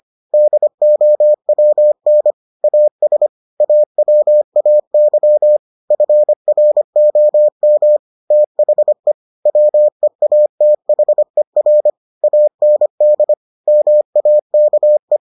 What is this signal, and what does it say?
Morse code, heard as a keyed tone, about 25 wpm, 600 Hz. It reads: DOWN AS AWAY FROM THE WEATHER AND MAKE